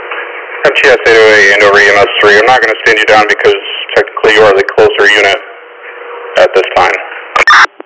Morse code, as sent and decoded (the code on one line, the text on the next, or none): none